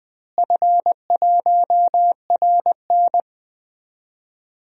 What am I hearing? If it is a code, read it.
Morse code, heard as a keyed tone, about 20 wpm, 700 Hz. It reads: F1RN
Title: DXCC-100 Call Signs - 4 Characters - Repeated 2x - 20wpm